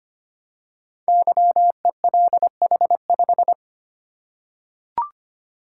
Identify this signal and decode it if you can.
Morse code, heard as a keyed tone, about 25 wpm, 700 Hz. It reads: YELH5
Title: Code Groups - 5 Letter:Numbers - 25wpm